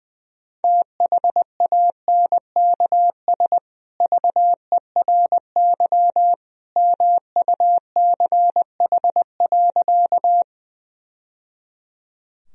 Morse code, heard as a keyed tone, about 21 wpm, 700 Hz. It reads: THANKS VERY MUCH.
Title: Sentences from Top 1000 Words - Review Part 3 - 20wpm